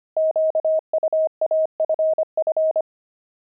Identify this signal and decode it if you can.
Morse code, heard as a keyed tone, about 25 wpm, 630 Hz. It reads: QUAFF
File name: pangram-fs-25-cs-25-saxophones